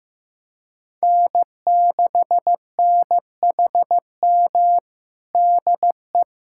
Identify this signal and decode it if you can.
Morse code, heard as a keyed tone, about 15 wpm, 700 Hz. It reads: N6NHM DE
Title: USA Callsign DE Callsign 15wpm